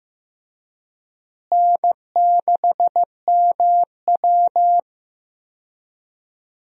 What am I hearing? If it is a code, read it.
Morse code, heard as a keyed tone, about 15 wpm, 700 Hz. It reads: N6MW
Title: USA Call Signs - 4 Characters - Repeated 3x - 15wpm